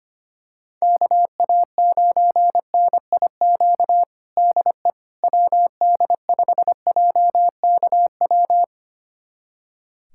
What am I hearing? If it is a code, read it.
Morse code, heard as a keyed tone, about 25 wpm, 700 Hz. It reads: KA9NIQ DE WD5JKW
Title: USA Callsign DE Callsign - 6 Characters - 25wpm